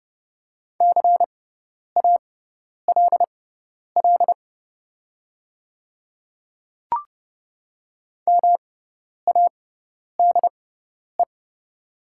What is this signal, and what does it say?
Morse code, heard as a keyed tone, about 31 wpm, spaced as for 10 wpm, 700 Hz. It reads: CALL MADE
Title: 6x Character Spacing - Top 300 Words - 30wpm